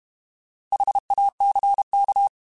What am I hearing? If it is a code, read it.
Morse code, heard as a keyed tone, about 32 wpm, 800 Hz. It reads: HACK
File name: Alphabetical / Alphabetical mp3s / Hs @30